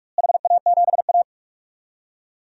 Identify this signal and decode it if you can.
Morse code, heard as a keyed tone, about 46 wpm, 690 Hz. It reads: HA7A
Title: DXCC-100 Call Signs - 4 Characters - Repeated 3x - 45wpm